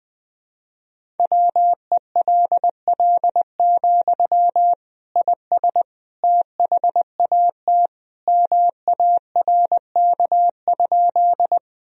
Morse code, heard as a keyed tone, about 20 wpm, 700 Hz. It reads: WELL, IS THAT MARK?
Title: Sentences from Top 1000 Words - Review Part 1 - 20wpm